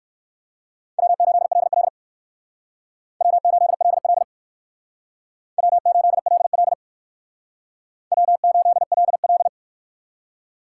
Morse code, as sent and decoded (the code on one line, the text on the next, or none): .-- ---.. .-.. .-.. / .-- ---.. .-.. .-.. / .-- ---.. .-.. .-.. / .-- ---.. .-.. .-..
W8LL W8LL W8LL W8LL